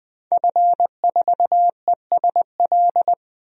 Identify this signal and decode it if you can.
Morse code, heard as a keyed tone, about 20 wpm, 700 Hz. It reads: F4ESL
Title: DXCC-100 Call Signs - 5 Characters - 20wpm